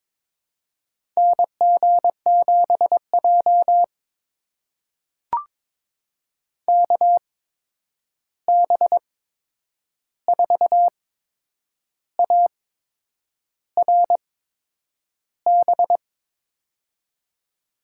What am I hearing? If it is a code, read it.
Morse code, heard as a keyed tone, about 22 wpm, spaced as for 8 wpm, 700 Hz. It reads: NG7J KB4ARB
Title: USA Call Signs - 8x Character Spacing - 22wpm